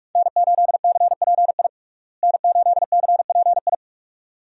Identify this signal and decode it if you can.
Morse code, heard as a keyed tone, about 47 wpm, 700 Hz. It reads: N8CPI N8CPI
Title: USA Call Signs - 5 Characters - Repeated 2x - 45wpm